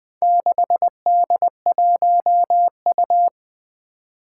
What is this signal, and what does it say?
Morse code, heard as a keyed tone, about 20 wpm, 700 Hz. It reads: 6D1U
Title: Four Challenging Letter-Number Combinations 20wpm